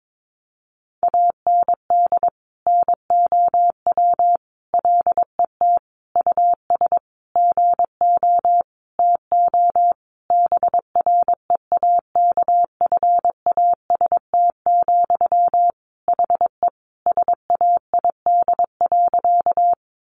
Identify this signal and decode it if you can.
Morse code, heard as a keyed tone, about 22 wpm, 700 Hz. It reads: AND NOW LET US GO TO BREAKFAST, HE SAID.
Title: Sentences from Top 1500 Words 22wpm